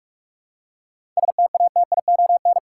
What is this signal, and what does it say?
Morse code, heard as a keyed tone, about 46 wpm, 700 Hz. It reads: STATION
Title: Seven Letter Words - 45wpm